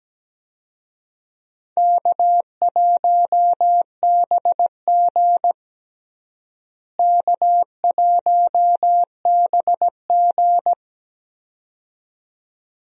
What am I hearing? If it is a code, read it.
Morse code, heard as a keyed tone, about 17 wpm, 700 Hz. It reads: K1BG K1BG
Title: USA Call Signs - 4 Characters - Repeated 2x - 17wpm